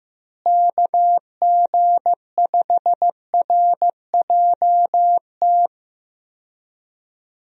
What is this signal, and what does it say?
Morse code, heard as a keyed tone, about 15 wpm, 700 Hz. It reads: KG5RJT
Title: USA Call Signs - 6 Characters - Repeated 2x - 15wpm